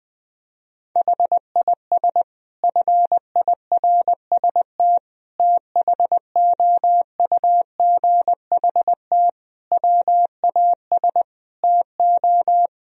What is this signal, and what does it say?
Morse code, heard as a keyed tone, about 20 wpm, 700 Hz. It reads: HIS FIRST THOUGHT WAS TO
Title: Sentences from Top 1000 Words - Review Part 3 - 20wpm